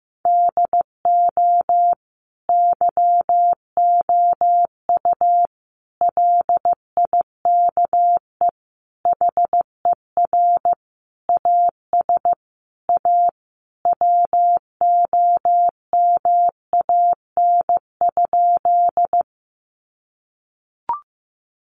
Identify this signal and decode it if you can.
Morse code, heard as a keyed tone, about 14 wpm, 700 Hz. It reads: DO YOU LIKE HER AS A WOMAN?
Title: Sentences from Top 1100 Words 15wpm